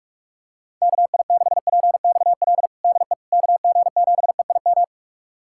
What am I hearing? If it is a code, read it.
Morse code, heard as a keyed tone, about 46 wpm, 700 Hz. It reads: KI6PXL DE KG7EIM
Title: USA Callsign DE Callsign 45wpm